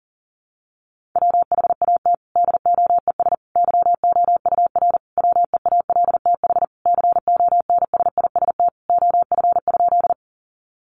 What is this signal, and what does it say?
Morse code, heard as a keyed tone, about 40 wpm, 700 Hz. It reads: WHAT DOES YOUR WEALTH CONSIST OF?
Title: Sentences from Top 1400 Words 40wpm